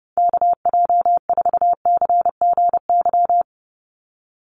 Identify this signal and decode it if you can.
Morse code, heard as a keyed tone, about 30 wpm, 700 Hz. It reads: KJ4CGY
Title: USA Call Signs 30wpm